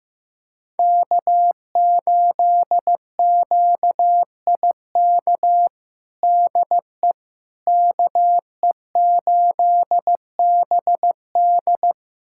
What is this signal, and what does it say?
Morse code, heard as a keyed tone, about 15 wpm, 700 Hz. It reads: K8QIK DE KE8BD
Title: USA Callsign DE Callsign - 5 Characters - 15wpm